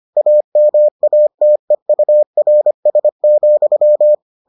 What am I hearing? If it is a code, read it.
Morse code, heard as a keyed tone, about 25 wpm, 600 Hz. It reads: AMATEURS,